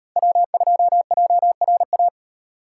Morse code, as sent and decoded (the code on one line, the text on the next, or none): .-- ..--- .--- .-. .-
W2JRA